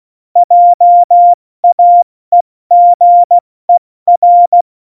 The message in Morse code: JAEGER

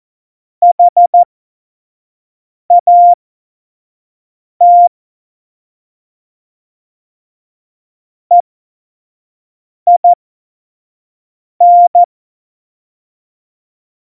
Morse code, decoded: HAT EIN

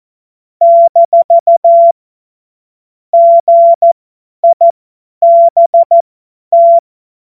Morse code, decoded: - GIBT

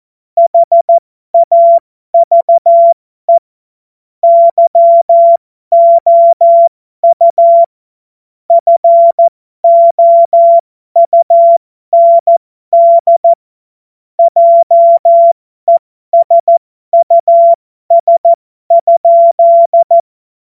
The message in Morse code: HAVE YOU FOUND JESUS?

